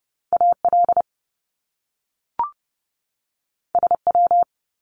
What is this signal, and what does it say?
Morse code, heard as a keyed tone, about 30 wpm, 700 Hz. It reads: AL SW
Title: Top Two Letter Combinations in Words 30wpm